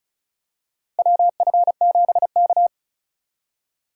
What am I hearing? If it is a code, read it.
Morse code, heard as a keyed tone, about 36 wpm, 700 Hz. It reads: WF7K
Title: USA Call Signs - 3x Character Spacing - 35wpm